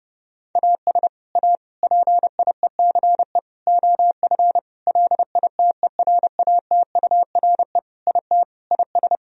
Morse code, heard as a keyed tone, about 30 wpm, 700 Hz. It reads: AS A PIECE OF LITERATURE IT IS
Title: Sentences from Top 1000 Words - Review Part 3 - 30wpm